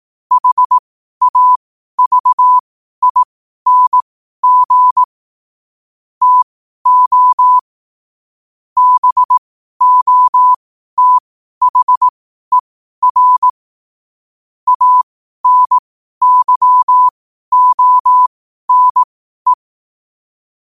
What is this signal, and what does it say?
Morse code, heard as a keyed tone, about 18 wpm, spaced as for 12 wpm, 1000 Hz. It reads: HAVING TO BOTHER ANYONE
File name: qotdfiles / 12wpm